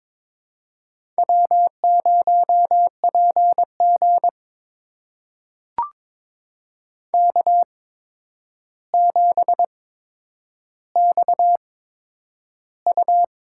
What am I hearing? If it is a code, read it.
Morse code, heard as a keyed tone, about 22 wpm, spaced as for 9 wpm, 700 Hz. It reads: W0PG K7XU
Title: USA Call Signs - 4 Characters - 8x Character Spacing - 22wpm